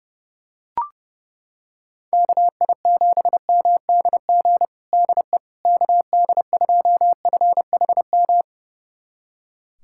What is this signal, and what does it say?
Morse code, heard as a keyed tone, about 30 wpm, 700 Hz. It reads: KI7MDG DE KD2FHM
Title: USA Callsign DE Callsign 30wpm